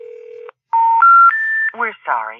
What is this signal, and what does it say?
Special information tone, with speech.